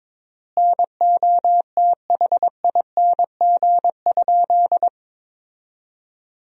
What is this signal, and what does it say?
Morse code, heard as a keyed tone, about 22 wpm, 700 Hz. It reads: NOTHING?
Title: 3x Word-Spacing - Sentences from Top 500 Words - 22wpm